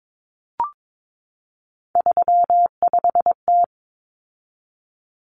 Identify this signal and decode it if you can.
Morse code, heard as a keyed tone, about 22 wpm, 700 Hz. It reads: 35T
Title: Common Cut Numbers - 3-Digit 22wpm